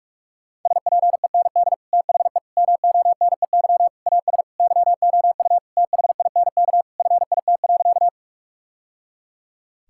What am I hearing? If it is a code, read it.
Morse code, heard as a keyed tone, about 41 wpm, 700 Hz. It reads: SPEND THE MONEY AS YOU THINK FIT.
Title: Sentences from Top 1000 Words - Review Part 5 - 45wpm